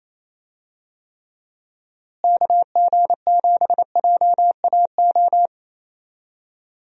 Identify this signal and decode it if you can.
Morse code, heard as a keyed tone, about 27 wpm, 700 Hz. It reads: KG7JAO
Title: USA Call Signs - 6 Characters - Repeated 3x - 28wpm